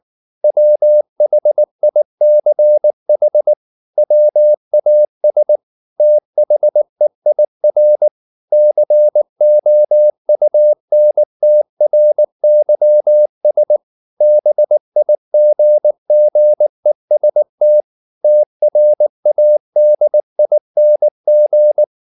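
Morse code, heard as a keyed tone, about 19 wpm, 600 Hz. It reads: WHICH WAS THEIR COUNTRYS BIGGEST TRADING
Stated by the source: Morse code at 19 wpm from CW med Gnister, reading Ken Follett's novel Never in English.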